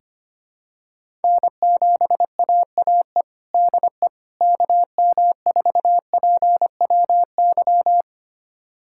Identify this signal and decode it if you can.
Morse code, heard as a keyed tone, about 25 wpm, 700 Hz. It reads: N7AAE DE KM4PWY
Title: USA Callsign DE Callsign 25wpm